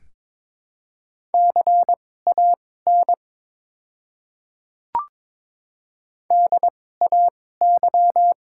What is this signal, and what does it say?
Morse code, heard as a keyed tone, about 23 wpm, spaced as for 16 wpm, 700 Hz. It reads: CAN DAY
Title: 2x Character Spacing - Top 200 Words - 22wpm